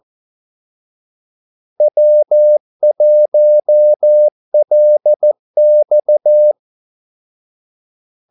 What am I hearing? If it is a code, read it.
Morse code, heard as a keyed tone, about 14 wpm, 600 Hz. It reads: W1LX
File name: Callsigns-14wpm